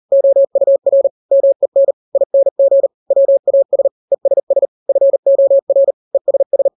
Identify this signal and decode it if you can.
Morse code, heard as a keyed tone, about 40 wpm, 550 Hz. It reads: OUR MEN ING WAS ESS FOR ESS
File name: n-grams / 3-grams 40 35 WPM